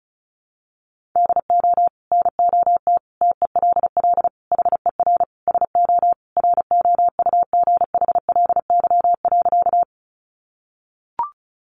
Morse code, heard as a keyed tone, about 34 wpm, 700 Hz. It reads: DO NOT TELL HER SO ROUGHLY.
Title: Sentences from Top 1500 Words 35wpm